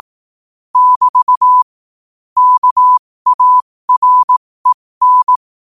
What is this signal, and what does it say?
Morse code, heard as a keyed tone, about 18 wpm, spaced as for 15 wpm, 1000 Hz. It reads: <BT> KAREN
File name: qotdfiles / 15wpm